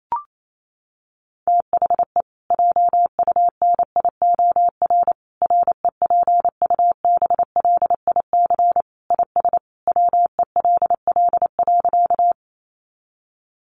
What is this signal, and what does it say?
Morse code, heard as a keyed tone, about 28 wpm, 700 Hz. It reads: THE JUNIOR REPUBLIC IS WELL.
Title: Sentences from Top 1500 Words 28wpm